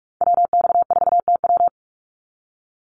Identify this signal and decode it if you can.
Morse code, heard as a keyed tone, about 44 wpm, 700 Hz. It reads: WX4TW
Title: NA SOTA Activator Callsigns 45wpm